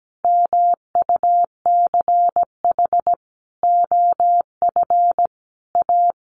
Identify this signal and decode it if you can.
Morse code, heard as a keyed tone, about 17 wpm, 700 Hz. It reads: MUCH OF A
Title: Sentences from Top 1400 Words 17wpm